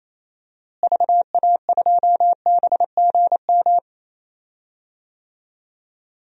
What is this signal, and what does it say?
Morse code, heard as a keyed tone, about 28 wpm, 700 Hz. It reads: VA2BGM
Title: DXCC-100 Call Signs - 6 Characters - Repeated 2x - 28wpm